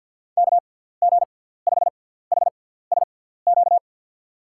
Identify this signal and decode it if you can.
Morse code, heard as a keyed tone, about 52 wpm, spaced as for 17 wpm, 700 Hz. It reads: KG5HSQ
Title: USA Call Signs - 6 Characters - 6x Character Spacing - 50wpm